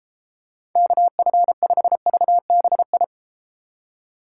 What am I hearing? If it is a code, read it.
Morse code, heard as a keyed tone, about 33 wpm, 700 Hz. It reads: KF5VBI